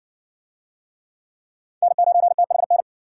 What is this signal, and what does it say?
Morse code, heard as a keyed tone, about 60 wpm, 700 Hz. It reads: N9THN